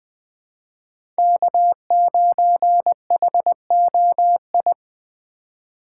Morse code, heard as a keyed tone, about 20 wpm, 700 Hz. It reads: K9HOI
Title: USA Call Signs - 4x Character Spacing - 20wpm